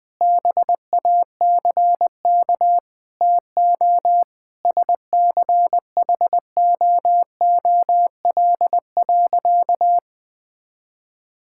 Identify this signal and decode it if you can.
Morse code, heard as a keyed tone, about 20 wpm, 700 Hz. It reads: BACK TO SCHOOL.
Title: Sentences with 5-Word Phrases - 20wpm